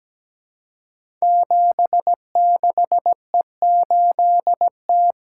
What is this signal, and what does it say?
Morse code, heard as a keyed tone, about 17 wpm, 700 Hz. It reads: 76E8T
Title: Code Groups - 5 Letter:Numbers - 17wpm